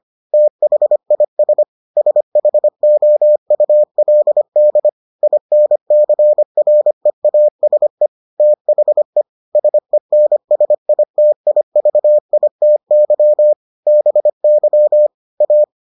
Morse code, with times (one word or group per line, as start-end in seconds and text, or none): 0.3-1.6 THIS
2.0-4.9 SHOULD
5.2-8.1 INCREASE
8.4-9.2 THE
9.6-13.5 SENSITIVITY
13.9-15.1 BY
15.4-15.6 A